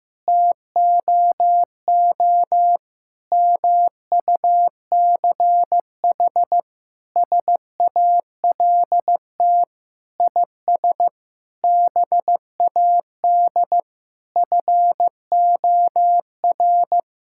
Morse code, heard as a keyed tone, about 16 wpm, 700 Hz. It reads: TOO MUCH SALT IS BAD FOR
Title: Sentences from Top 1000 Words - Review Part 5 - 15wpm